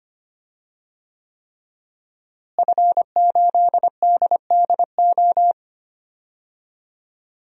Morse code, heard as a keyed tone, about 25 wpm, 700 Hz. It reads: F8DDO
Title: DXCC-100 Call Signs - 5 Characters - Repeated 3x - 25wpm